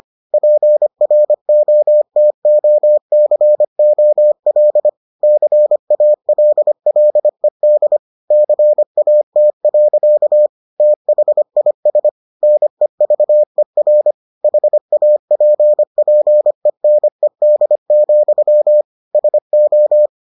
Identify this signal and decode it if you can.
Morse code, heard as a keyed tone, about 25 wpm, 600 Hz. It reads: PROTOCOL CALLED CAT. THIS NEVER HAPPENED, SO